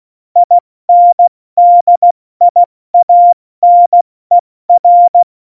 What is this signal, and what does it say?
Morse code, heard as a keyed tone, about 16 wpm, spaced as for 14 wpm, 700 Hz. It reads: INDIANER
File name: Morsekurs / Witze / Witz 05. September 17, Tempo 70